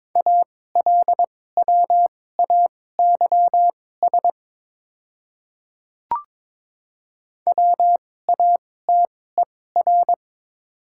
Morse code, heard as a keyed tone, about 21 wpm, spaced as for 16 wpm, 700 Hz. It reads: ALWAYS WATER
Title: 2x Character Spacing - Top 300 Words - 22wpm